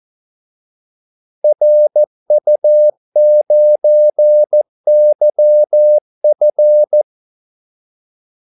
Morse code, heard as a keyed tone, about 14 wpm, 600 Hz. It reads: RU9YF
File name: Callsigns-14wpm